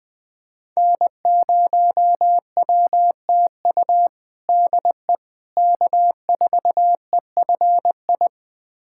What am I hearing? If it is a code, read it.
Morse code, heard as a keyed tone, about 20 wpm, 700 Hz. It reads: N0WTU DE K4EFI
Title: USA Callsign DE Callsign - 5 Characters - 20wpm